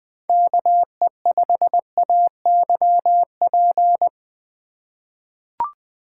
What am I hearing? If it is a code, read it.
Morse code, heard as a keyed tone, about 20 wpm, 700 Hz. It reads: KE5AYP
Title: USA Call Signs - 6 Characters - 2x Character Spacing - 20wpm